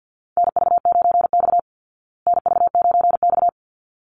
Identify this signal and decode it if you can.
Morse code, heard as a keyed tone, about 50 wpm, 700 Hz. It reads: NV9X NV9X
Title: Call Signs Repeated 3x - 50wpm